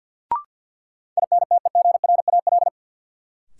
Morse code, heard as a keyed tone, about 53 wpm, 700 Hz. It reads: INTEGRAL
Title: Sets of 5 Words - New Word Drill - 50wpm